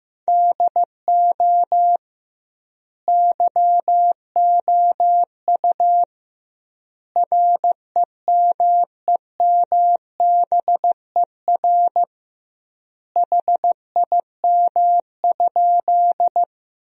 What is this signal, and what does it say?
Morse code, heard as a keyed tone, about 16 wpm, 700 Hz. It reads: DO YOU REMEMBER HIM?
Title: 2x Word-Spacing - Sentences from Top 400 Words - 15wpm